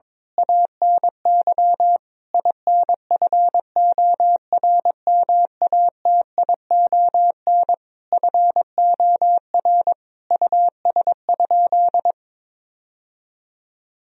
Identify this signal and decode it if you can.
Morse code, heard as a keyed tone, about 22 wpm, 700 Hz. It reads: ANY INFORMATION FOR US?
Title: Sentences from Top 1000 Words - Review Part 3 - 22wpm